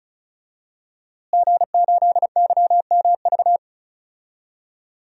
Morse code, heard as a keyed tone, about 36 wpm, 700 Hz. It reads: G8YMV
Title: DXCC-100 Call Signs - 5 Characters - Repeated 2x - 35wpm